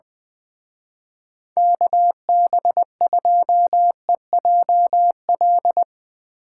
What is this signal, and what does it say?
Morse code, heard as a keyed tone, about 21 wpm, 700 Hz. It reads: KB2EJL
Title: Speed Racing - USA 2x3 Call Signs 20wpm